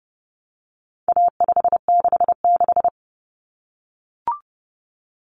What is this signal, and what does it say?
Morse code, heard as a keyed tone, about 30 wpm, 700 Hz. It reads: A566
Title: Common Cut Numbers - 4-Digit 30wpm